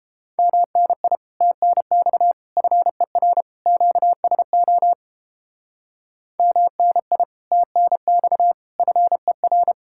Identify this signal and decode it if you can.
Morse code, heard as a keyed tone, about 29 wpm, 700 Hz. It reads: MNI TNX FER QSO MNI TNX FER